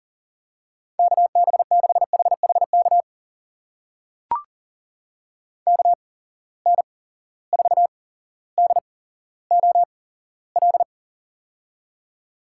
Morse code, heard as a keyed tone, about 43 wpm, spaced as for 17 wpm, 700 Hz. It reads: KB6HHK KN4DOL
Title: USA Call Signs - 8x Character Spacing - 40wpm